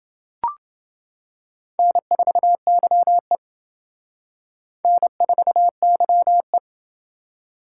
Morse code, heard as a keyed tone, about 28 wpm, 700 Hz. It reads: N4YE N4YE